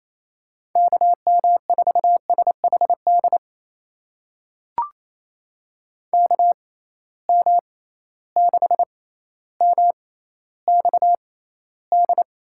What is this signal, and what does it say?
Morse code, heard as a keyed tone, about 28 wpm, spaced as for 14 wpm, 700 Hz. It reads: KM4SHD KM6MXD
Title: USA Call Signs - 6 Characters - 6x Character Spacing - 28wpm